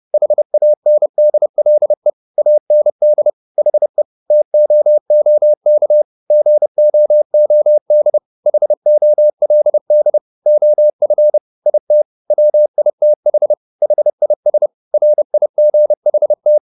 Morse code, heard as a keyed tone, about 30 wpm, 600 Hz. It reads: HANDLE AND HE TOOK GOOD HOLD OF IT WITH HIS RIGHT